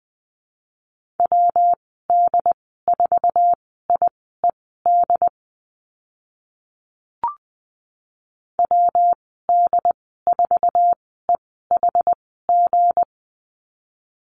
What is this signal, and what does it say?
Morse code, heard as a keyed tone, about 20 wpm, spaced as for 14 wpm, 700 Hz. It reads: WD4IED WD4EHG
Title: 173 Lesson 28g - Number 9 - Call Signs 20wpm